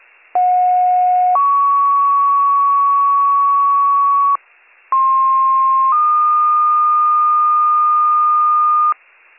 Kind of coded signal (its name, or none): two-tone page